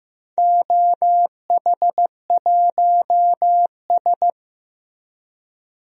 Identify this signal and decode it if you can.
Morse code, heard as a keyed tone, about 15 wpm, 700 Hz. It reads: OH1S
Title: DXCC-100 Call Signs - 4 Characters - Repeated 3x - 15wpm